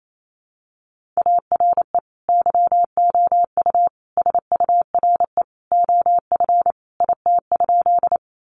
Morse code, heard as a keyed tone, about 28 wpm, 700 Hz. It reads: ARE YOU SURE OF IT?
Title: Sentences from Top 300 Words 28wpm